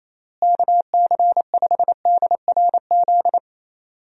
Morse code, heard as a keyed tone, about 28 wpm, 700 Hz. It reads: KC5DRZ
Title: USA Call Signs - 6 Characters - Repeated 2x - 28wpm